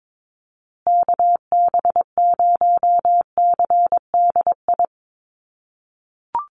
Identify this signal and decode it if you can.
Morse code, heard as a keyed tone, about 22 wpm, 700 Hz. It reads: KB0CDI
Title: USA Call Signs Repeated 2x 22wpm